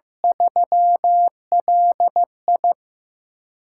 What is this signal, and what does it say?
Morse code, heard as a keyed tone, about 15 wpm, 700 Hz. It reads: 3LI
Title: USA Call Signs - 6x Character Spacing - 15wpm